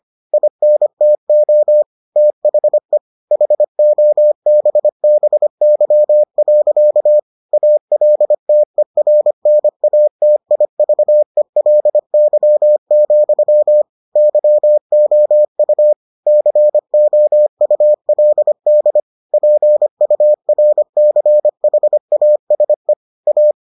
Morse code, heard as a keyed tone, about 25 wpm, 600 Hz. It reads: INTO THE HOBBY. ALTERNATIVELY, YOU COULD PURCHASE A